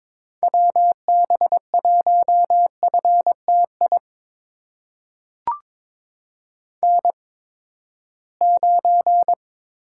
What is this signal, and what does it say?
Morse code, heard as a keyed tone, about 21 wpm, spaced as for 15 wpm, 700 Hz. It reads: WB1FTI N9
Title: USA Call Signs - 8x Character Spacing - 22wpm